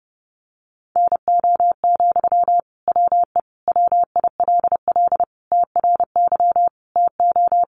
Morse code, heard as a keyed tone, about 30 wpm, 700 Hz. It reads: NO, WE WILL TRY TO
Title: Sentences from Top 1100 Words 30wpm